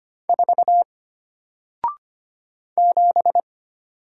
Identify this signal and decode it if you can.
Morse code, heard as a keyed tone, about 25 wpm, 700 Hz. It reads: 4 7
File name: Single-Letter-Number-25wpm